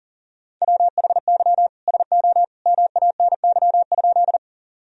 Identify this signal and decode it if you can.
Morse code, heard as a keyed tone, about 37 wpm, 700 Hz. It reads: WHY SO MANY?
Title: Sentences from Top 1000 Words - Review Part 1 - 40wpm